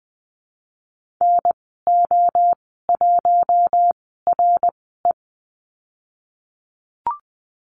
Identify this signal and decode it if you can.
Morse code, heard as a keyed tone, about 19 wpm, spaced as for 14 wpm, 700 Hz. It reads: NO1RE
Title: 055 Lesson 10g - Letter r - Call Signs 20wpm